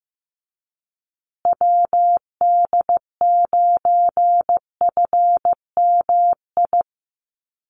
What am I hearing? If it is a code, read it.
Morse code, heard as a keyed tone, about 15 wpm, 700 Hz. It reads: WD9FMI
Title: USA Call Signs Repeated 3x 15wpm